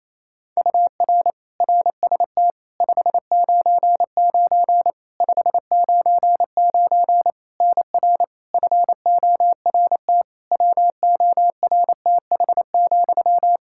Morse code, heard as a keyed tone, about 28 wpm, 700 Hz. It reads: UR RST 599 599 NR FORT WORTH,
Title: Practice QSO - Exchange 1 - Beginning - 28wpm